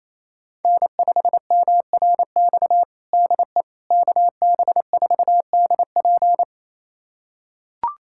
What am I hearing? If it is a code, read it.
Morse code, heard as a keyed tone, about 28 wpm, 700 Hz. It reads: N5MRX DE KB4DP
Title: USA Callsign DE Callsign - 5 Characters - 28wpm